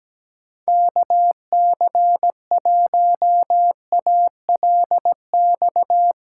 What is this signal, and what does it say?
Morse code, heard as a keyed tone, about 17 wpm, 700 Hz. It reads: KC1ALX
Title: USA Callsign DE Callsign - 6 Characters - 17wpm